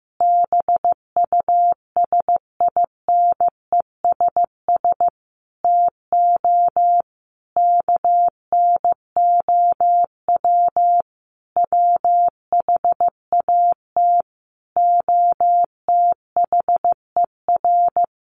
Morse code, heard as a keyed tone, about 15 wpm, 700 Hz. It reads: BUSINESS TO KNOW WHAT OTHER
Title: Sentences from Top 600 Words 15wpm